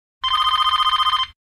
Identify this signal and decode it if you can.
There is no code.